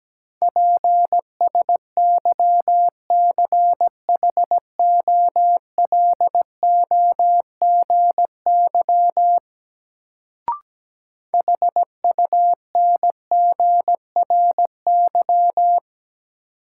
Morse code, heard as a keyed tone, about 17 wpm, 700 Hz. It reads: PSYCHOLOGY HUNGRY
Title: Sentences from Top 1400 Words - New Word Drill - 17wpm